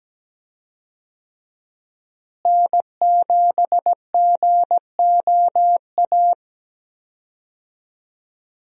Morse code, heard as a keyed tone, about 17 wpm, 700 Hz. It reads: N7GOA
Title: USA Call Signs - 5 Characters - Repeated 2x - 17wpm